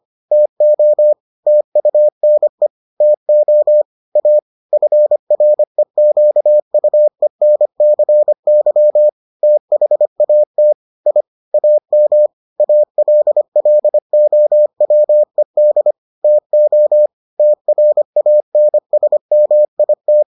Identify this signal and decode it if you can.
Morse code, heard as a keyed tone, about 25 wpm, 600 Hz. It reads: TO TUNE TO A FREQUENCY THAT I AM ALLOWED TO TRANSMIT